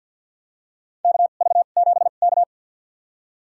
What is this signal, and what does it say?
Morse code, heard as a keyed tone, about 52 wpm, spaced as for 37 wpm, 700 Hz. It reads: KV7K
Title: USA Call Signs - 2x Character Spacing - 50wpm